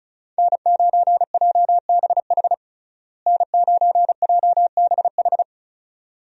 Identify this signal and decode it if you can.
Morse code, heard as a keyed tone, about 34 wpm, 700 Hz. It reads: N9JBH N9JBH
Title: USA Call Signs - 5 Characters - Repeated 2x - 35wpm